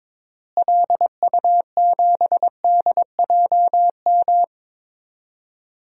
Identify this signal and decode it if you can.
Morse code, heard as a keyed tone, about 22 wpm, 700 Hz. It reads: LU7DJM
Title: DXCC-100 Call Signs - 6 Characters - Repeated 2x - 22wpm